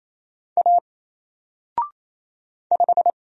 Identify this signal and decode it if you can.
Morse code, heard as a keyed tone, about 28 wpm, 700 Hz. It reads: A 5
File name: Single-Letter-Number-28wpm